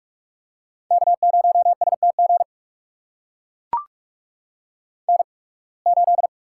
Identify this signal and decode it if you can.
Morse code, heard as a keyed tone, about 44 wpm, spaced as for 29 wpm, 700 Hz. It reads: K0STG N8
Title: USA Call Signs - 8x Character Spacing - 45wpm